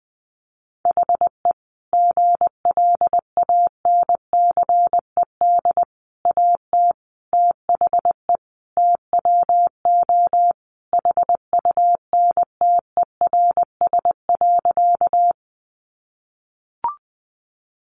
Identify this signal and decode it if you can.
Morse code, heard as a keyed tone, about 20 wpm, 700 Hz. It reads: HE GLANCED AT THE TWO HUNTERS.
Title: A Sound Of Thunder 20wpm